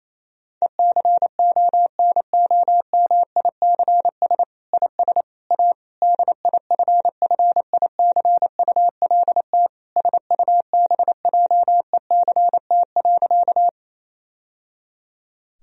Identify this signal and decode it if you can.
Morse code, heard as a keyed tone, about 28 wpm, 700 Hz. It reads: ECONOMICS IS A DIFFICULT SUBJECT.
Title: Sentences from Top 1000 Words - Review Part 3 - 28wpm